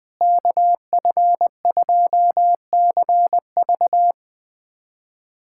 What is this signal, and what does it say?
Morse code, heard as a keyed tone, about 21 wpm, 700 Hz. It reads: KF2CV
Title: USA Callsign DE Callsign - 5 Characters - 20wpm